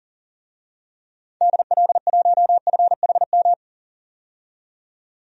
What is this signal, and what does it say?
Morse code, heard as a keyed tone, about 40 wpm, 700 Hz. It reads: DL1FHM
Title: DXCC-100 Call Signs - 6 Characters - 40wpm